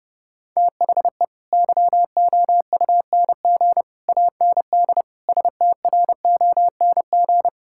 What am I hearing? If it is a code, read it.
Morse code, heard as a keyed tone, about 30 wpm, 700 Hz. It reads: THE YOUNG AND STRONG
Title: Sentences from Top 1000 Words - Review Part 5 - 30wpm